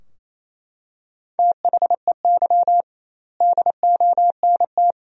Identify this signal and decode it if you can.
Morse code, heard as a keyed tone, about 28 wpm, 700 Hz. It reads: THEY DONT
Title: 2x Word-Spacing - Sentences from Top 300 Words - 28wpm